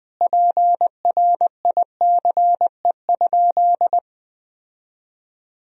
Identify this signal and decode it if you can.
Morse code, heard as a keyed tone, about 20 wpm, 700 Hz. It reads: PRICE?
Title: Sentences from Top 1000 Words - Review Part 3 - 20wpm